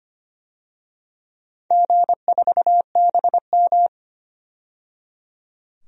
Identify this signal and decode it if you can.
Morse code, heard as a keyed tone, about 25 wpm, 700 Hz. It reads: G4BM
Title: DXCC-100 Call Signs - 4 Characters - 25wpm